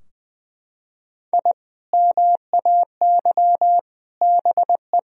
Morse code, heard as a keyed tone, about 20 wpm, 700 Hz. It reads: I MAY BE
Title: Sentences from Top 1000 Words - Review Part 2 - 20wpm